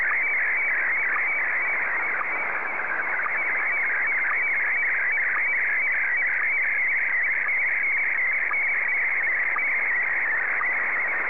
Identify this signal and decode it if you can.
SSTV (slow-scan television), Scottie DX.